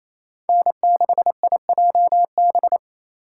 Morse code, heard as a keyed tone, about 34 wpm, 700 Hz. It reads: N6IJB
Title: Speed Racing - USA 1x3 Call Signs 28wpm